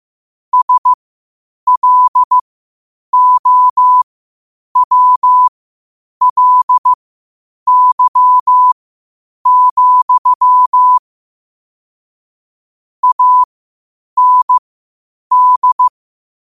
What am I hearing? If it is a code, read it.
Morse code, heard as a keyed tone, about 15 wpm, spaced as for 8 wpm, 1000 Hz. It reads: SLOWLY, AND
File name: qotdfiles / 8wpm